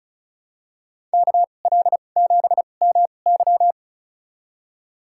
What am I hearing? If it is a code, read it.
Morse code, heard as a keyed tone, about 36 wpm, spaced as for 25 wpm, 700 Hz. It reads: KL7MY
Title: USA Call Signs - 5 Characters - 2x Character Spacing - 35wpm